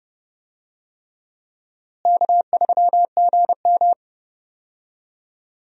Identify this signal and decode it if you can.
Morse code, heard as a keyed tone, about 31 wpm, 700 Hz. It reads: K3GM
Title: USA Call Signs - 4 Characters - Repeated 3x - 30wpm